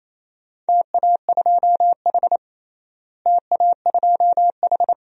Morse code, heard as a keyed tone, about 27 wpm, 700 Hz. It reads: TA2H TA2H
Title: DXCC-100 Call Signs - 4 Characters - Repeated 2x - 28wpm